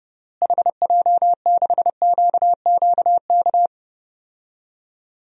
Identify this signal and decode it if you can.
Morse code, heard as a keyed tone, about 30 wpm, 700 Hz. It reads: HJ6QQK
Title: DXCC-100 Call Signs - 6 Characters - Repeated 3x - 30wpm